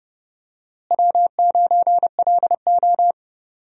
Morse code, heard as a keyed tone, about 30 wpm, 700 Hz. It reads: W9LO
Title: USA Call Signs - 4 Characters - Repeated 2x - 30wpm